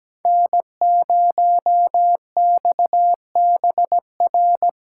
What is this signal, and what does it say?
Morse code, heard as a keyed tone, about 17 wpm, 700 Hz. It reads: N0XBR
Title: USA Callsign DE Callsign - 5 Characters - 17wpm